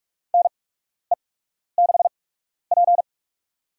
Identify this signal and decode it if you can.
Morse code, heard as a keyed tone, about 42 wpm, spaced as for 12 wpm, 700 Hz. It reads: NE6P
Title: USA Call Signs - 4 Characters - 8x Character Spacing - 45wpm